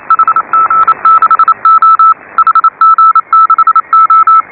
Morse code, heard as a keyed tone, about 32 wpm, 1310 Hz. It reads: HG6O HG6O
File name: AUDIO2018 / HA5BA CW